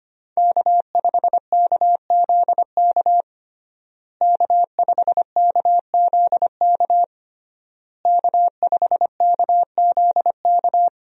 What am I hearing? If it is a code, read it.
Morse code, heard as a keyed tone, about 25 wpm, 700 Hz. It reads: K5KZK K5KZK K5KZK
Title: USA Call Signs - 5 Characters - Repeated 3x - 25wpm